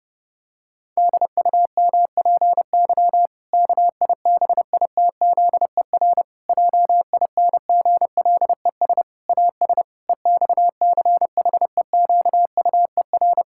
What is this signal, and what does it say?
Morse code, heard as a keyed tone, about 30 wpm, 700 Hz. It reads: DUMPY KIBITZER JINGLES AS EXCHEQUER
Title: Pangram Challenge v2 - 30wpm